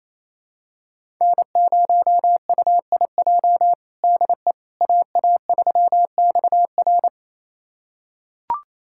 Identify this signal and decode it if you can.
Morse code, heard as a keyed tone, about 27 wpm, 700 Hz. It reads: N0UIJ DE AA3XR
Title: USA Callsign DE Callsign - 5 Characters - 28wpm